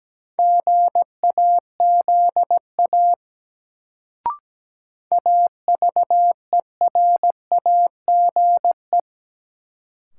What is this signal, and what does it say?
Morse code, heard as a keyed tone, about 17 wpm, 700 Hz. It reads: GAZA AVERAGE
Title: Sets of 5 Words - New Word Drill - 17wpm